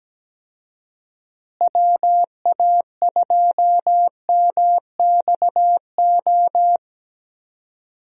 Morse code, heard as a keyed tone, about 17 wpm, 700 Hz. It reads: WA2MXO
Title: USA Call Signs - 6 Characters - Repeated 2x - 17wpm